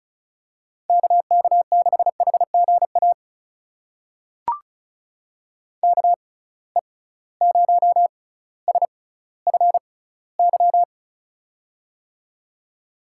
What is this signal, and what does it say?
Morse code, heard as a keyed tone, about 36 wpm, spaced as for 18 wpm, 700 Hz. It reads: KK6HGA KE0SFY
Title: USA Call Signs - 6 Characters - 6x Character Spacing - 35wpm